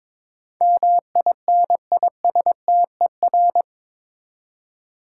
Morse code, heard as a keyed tone, about 21 wpm, 700 Hz. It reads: MINISTER
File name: practice / 22wpm / Eight Letter Words - 22wpm